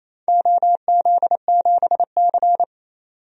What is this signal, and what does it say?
Morse code, heard as a keyed tone, about 27 wpm, 700 Hz. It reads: OZ7C
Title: DXCC-100 Call Signs - 4 Characters - 28wpm